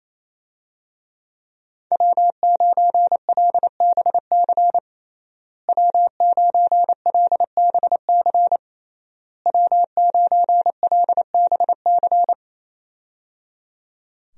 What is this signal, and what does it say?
Morse code, heard as a keyed tone, about 28 wpm, 700 Hz. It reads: W9LBC W9LBC W9LBC
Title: USA Call Signs - 5 Characters - Repeated 3x - 28wpm